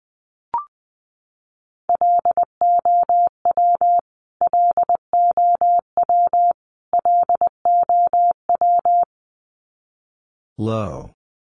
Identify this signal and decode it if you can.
Morse code, heard as a keyed tone, about 20 wpm, 700 Hz. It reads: LOW LOW LOW
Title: Top 200 Words Repeated 20wpm